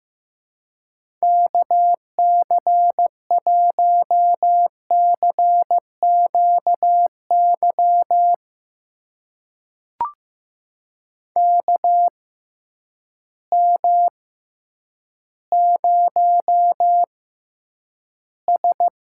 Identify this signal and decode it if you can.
Morse code, heard as a keyed tone, about 15 wpm, spaced as for 9 wpm, 700 Hz. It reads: KC1CQY KM0S
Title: USA Call Signs - 6 Characters - 6x Character Spacing - 15wpm